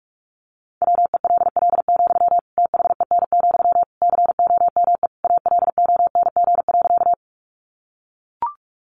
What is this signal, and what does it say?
Morse code, heard as a keyed tone, about 44 wpm, 700 Hz. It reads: WELL, THEN, COME ALONG.
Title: Sentences from Top 700 Words 45wpm